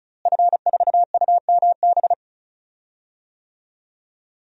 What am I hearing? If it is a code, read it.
Morse code, heard as a keyed tone, about 34 wpm, 700 Hz. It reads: F4UMB
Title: DXCC-100 Call Signs - 5 Characters - Repeated 3x - 35wpm